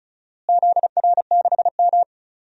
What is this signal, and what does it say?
Morse code, heard as a keyed tone, about 36 wpm, 700 Hz. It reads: ZR6M
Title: DXCC-100 Call Signs - 4 Characters - Repeated 2x - 35wpm